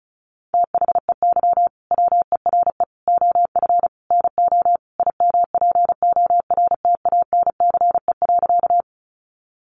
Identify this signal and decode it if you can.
Morse code, heard as a keyed tone, about 34 wpm, 700 Hz. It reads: THEY WERE OF NO IMPORTANCE.
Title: Sentences from Top 1200 Words 35wpm